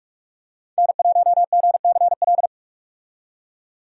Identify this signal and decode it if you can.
Morse code, heard as a keyed tone, about 46 wpm, 700 Hz. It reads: N1GCL
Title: USA Call Signs - 5 Characters - 45wpm